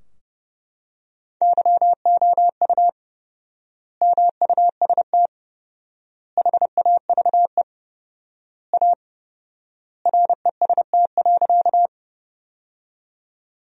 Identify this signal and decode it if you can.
Morse code, heard as a keyed tone, about 30 wpm, 700 Hz. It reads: YOU MUST HAVE A REST.
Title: 4x Word-Spacing - Sentences from Top 500 Words - 30wpm